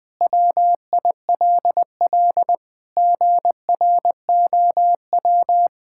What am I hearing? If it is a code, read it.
Morse code, heard as a keyed tone, about 21 wpm, 700 Hz. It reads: WILL GROW
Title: Sentences from Top 1600 Words 20wpm